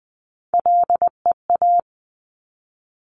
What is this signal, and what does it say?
Morse code, heard as a keyed tone, about 20 wpm, 700 Hz. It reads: LEA
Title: Top Three Letter Combinations in Words 20wpm